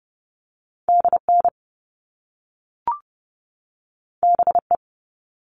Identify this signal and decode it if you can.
Morse code, heard as a keyed tone, about 30 wpm, 700 Hz. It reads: DN BE
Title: All Cut Numbers - 2-Digit 30wpm